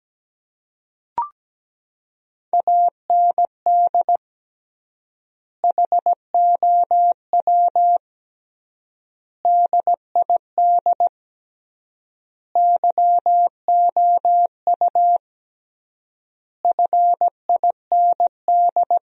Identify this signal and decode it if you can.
Morse code, heard as a keyed tone, about 17 wpm, 700 Hz. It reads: AND HOW DID YOU FIND
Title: 3x Word-Spacing - Sentences from Top 100 Words - 17wpm